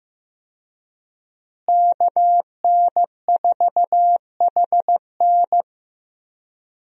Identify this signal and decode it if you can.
Morse code, heard as a keyed tone, about 15 wpm, 700 Hz. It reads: KN4HN
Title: USA Call Signs - 5 Characters - Repeated 3x - 15wpm